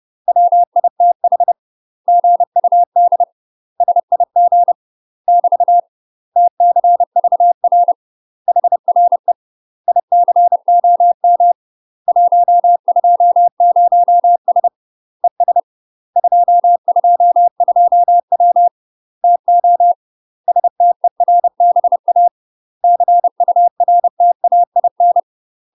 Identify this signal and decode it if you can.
Morse code, heard as a keyed tone, about 30 wpm, 700 Hz. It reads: WITH GUD SIG <BT> TCVR HRE ICOM 120S ES 222W TO STERBA CURTAIN
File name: UK MASTER QSOs TRAINING / 30WPM / UK MASTER QSOs 0020